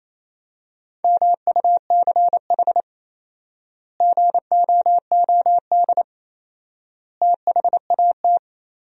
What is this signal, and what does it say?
Morse code, heard as a keyed tone, about 28 wpm, 700 Hz. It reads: MUCH GOOD THAT
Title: 4x Word-Spacing - Sentences from Top 200 Words - 28wpm